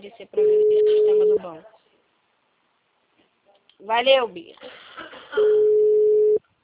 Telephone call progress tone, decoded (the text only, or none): ringback tone